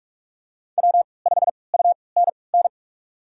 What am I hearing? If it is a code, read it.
Morse code, heard as a keyed tone, about 46 wpm, spaced as for 26 wpm, 700 Hz. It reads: W5UNN
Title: USA Call Signs - 5 Characters - 3x Character Spacing - 45wpm